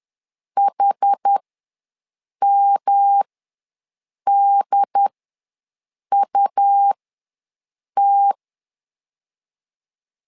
Morse code, heard as a keyed tone, about 10 wpm, spaced as for 6 wpm, 800 Hz. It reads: HMDUT
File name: LEZIONI / 3 MWRO / 3 M ws 10wpm 2121